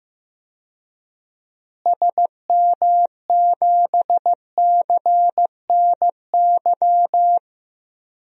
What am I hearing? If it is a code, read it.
Morse code, heard as a keyed tone, about 15 wpm, 700 Hz. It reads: SM7CNY
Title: DXCC-100 Call Signs - 6 Characters - Repeated 3x - 15wpm